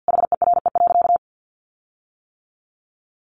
Morse code, heard as a keyed tone, about 50 wpm, 700 Hz. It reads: HERE.
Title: Sentences from Top 1300 Words 50wpm